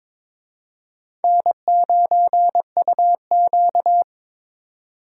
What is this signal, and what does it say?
Morse code, heard as a keyed tone, about 26 wpm, 700 Hz. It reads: N9UQ
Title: Speed Racing - USA 1x2 Call Signs 22wpm